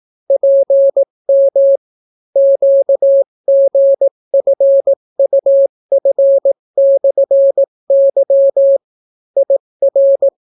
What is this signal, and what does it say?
Morse code, heard as a keyed tone, about 18 wpm, 550 Hz. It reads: PM QGFUF/Y IR